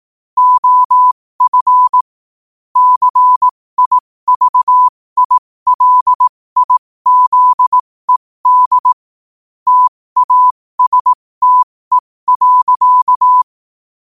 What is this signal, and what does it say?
Morse code, heard as a keyed tone, about 18 wpm, spaced as for 15 wpm, 1000 Hz. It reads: OF CIVILIZED TASTE.
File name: qotdfiles / 15wpm